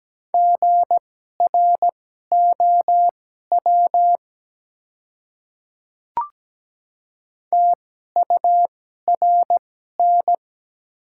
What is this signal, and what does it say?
Morse code, heard as a keyed tone, about 17 wpm, spaced as for 12 wpm, 700 Hz. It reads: GROW TURN
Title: 2x Character Spacing - Top 300 Words - 17wpm